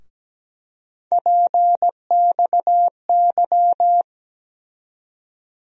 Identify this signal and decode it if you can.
Morse code, heard as a keyed tone, about 17 wpm, 700 Hz. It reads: PXY
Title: USA Call Sign Postfixes - 3 Letters 17wpm